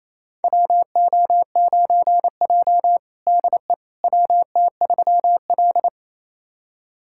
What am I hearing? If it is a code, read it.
Morse code, heard as a keyed tone, about 28 wpm, 700 Hz. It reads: WO9J DE WT3L
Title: USA Callsign DE Callsign - 4 Characters - 28wpm